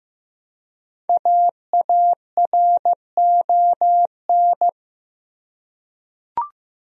Morse code, heard as a keyed tone, about 15 wpm, 700 Hz. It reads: AARON
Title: USA Common Names - 15wpm